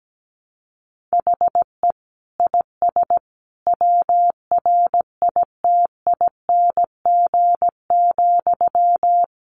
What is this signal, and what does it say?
Morse code, heard as a keyed tone, about 17 wpm, 700 Hz. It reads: HE IS WRITING,
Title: Sentences from Top 800 Words 17wpm